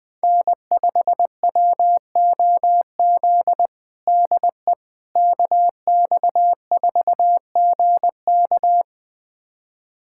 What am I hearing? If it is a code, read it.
Morse code, heard as a keyed tone, about 21 wpm, 700 Hz. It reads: N5WOZ DE KX4GK
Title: USA Callsign DE Callsign - 5 Characters - 20wpm